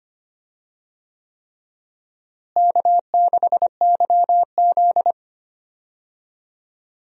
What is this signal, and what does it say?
Morse code, heard as a keyed tone, about 25 wpm, 700 Hz. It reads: K6YZ
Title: USA Call Signs - 4 Characters - Repeated 3x - 25wpm